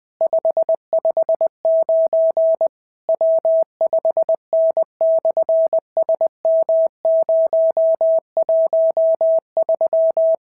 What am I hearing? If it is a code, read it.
Morse code, heard as a keyed tone, about 20 wpm, 650 Hz. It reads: 559 W5N/SM013
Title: SOTA QSO Practice 20wpm